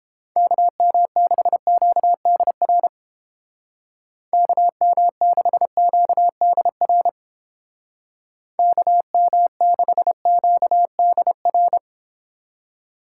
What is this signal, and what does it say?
Morse code, heard as a keyed tone, about 30 wpm, 700 Hz. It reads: KM6QDR KM6QDR KM6QDR